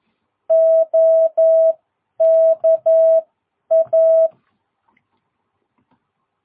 Morse code, heard as a keyed tone, about 11 wpm, spaced as for 9 wpm, 660 Hz. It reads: OKA
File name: jh8cbh / 2022cw